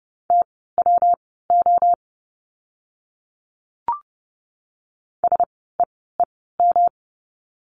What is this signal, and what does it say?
Morse code, heard as a keyed tone, about 30 wpm, spaced as for 17 wpm, 700 Hz. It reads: TWO SEEM
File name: course / 30:10wpm / 115 Lesson 19f - Letter w - Words 30wpm